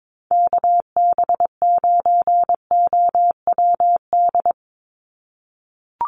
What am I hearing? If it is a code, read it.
Morse code, heard as a keyed tone, about 22 wpm, 700 Hz. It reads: KB9OWD
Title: Call Signs - 22wpm